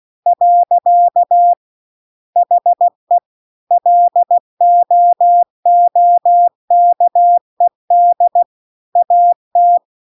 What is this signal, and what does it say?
Morse code, heard as a keyed tone, about 16 wpm, 700 Hz. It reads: . HE LOOKED AT